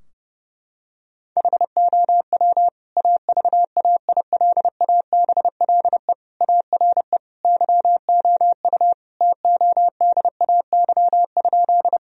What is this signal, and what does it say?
Morse code, heard as a keyed tone, about 30 wpm, 700 Hz. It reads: HOW AVAILABLE ARE YOU TODAY?
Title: Sentences from Top 1000 Words - Review Part 3 - 30wpm